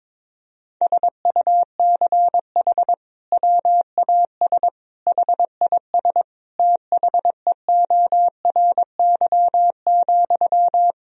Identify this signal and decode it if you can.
Morse code, heard as a keyed tone, about 22 wpm, 700 Hz. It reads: SUCH WAS HIS THEORY,
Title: Sentences from Top 1000 Words - Review Part 4 - 22wpm